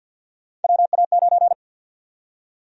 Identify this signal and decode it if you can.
Morse code, heard as a keyed tone, about 49 wpm, 700 Hz. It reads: WA9
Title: USA Call Sign Prefixes - 50wpm